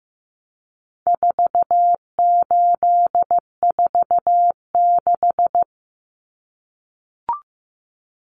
Morse code, heard as a keyed tone, about 15 wpm, 700 Hz. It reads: 4846
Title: Common Cut Numbers - 4-Digit 15wpm